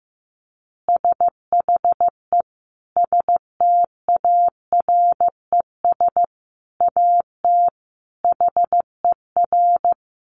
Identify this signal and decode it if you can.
Morse code, heard as a keyed tone, about 15 wpm, 700 Hz. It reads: SHE STARES AT HER
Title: Easy Phrases 15wpm